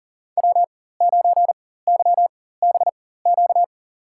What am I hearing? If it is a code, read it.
Morse code, heard as a keyed tone, about 40 wpm, spaced as for 19 wpm, 700 Hz. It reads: W9YBQ
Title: USA Call Signs - 5 Characters - 4x Character Spacing - 40wpm